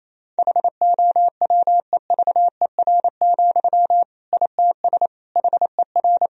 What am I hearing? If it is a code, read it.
Morse code, heard as a keyed tone, about 28 wpm, 700 Hz. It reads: HOWEVER, ITS HER
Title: Sentences from Top 1000 Words - Review Part 1 - 28wpm